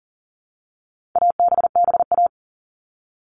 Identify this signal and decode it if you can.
Morse code, heard as a keyed tone, about 40 wpm, 700 Hz. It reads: ABBA
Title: All Cut Numbers - 4-Digit 40wpm